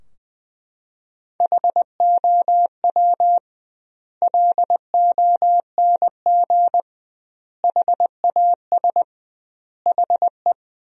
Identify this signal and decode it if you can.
Morse code, heard as a keyed tone, about 21 wpm, 700 Hz. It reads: HOW LONG HAS HE
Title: 2x Word-Spacing - Sentences from Top 100 Words - 20wpm